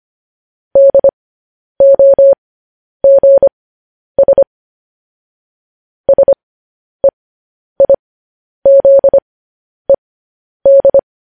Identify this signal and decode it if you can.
Morse code, heard as a keyed tone, about 25 wpm, spaced as for 10 wpm, 550 Hz. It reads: DOGS SEIZED
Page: news-headlines-in-morse-code-at-10-wpm